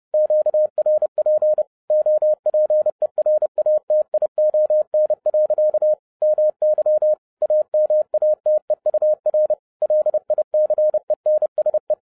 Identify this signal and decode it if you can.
Morse code, heard as a keyed tone, about 30 wpm, 610 Hz. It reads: QRP OPERATION. MY AMATEUR LICENSE
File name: 2011-07-mp3 / n40 EXCITING QRP CW OPERATION 30wpm